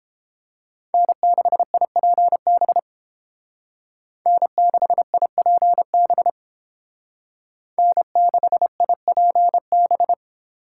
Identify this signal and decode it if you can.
Morse code, heard as a keyed tone, about 31 wpm, 700 Hz. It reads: N6IPB N6IPB N6IPB